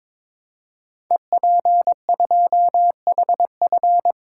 Morse code, heard as a keyed tone, about 22 wpm, 700 Hz. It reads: EP2HF
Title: Code Groups - 5 Letter:Numbers - 22wpm